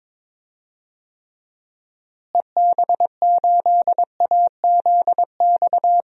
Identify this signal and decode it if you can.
Morse code, heard as a keyed tone, about 22 wpm, 700 Hz. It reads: EB8AZX
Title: DXCC-100 Call Signs - 6 Characters - Repeated 2x - 22wpm